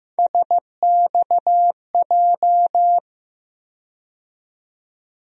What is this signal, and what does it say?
Morse code, heard as a keyed tone, about 15 wpm, 700 Hz. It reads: SXJ
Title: USA Call Sign Postfixes - 3 Letters 15wpm